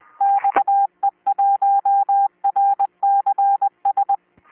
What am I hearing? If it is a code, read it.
Morse code, heard as a keyed tone, about 20 wpm, 800 Hz. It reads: XE1RCS